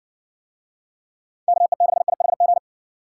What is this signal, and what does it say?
Morse code, heard as a keyed tone, about 58 wpm, 700 Hz. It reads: KE6IHG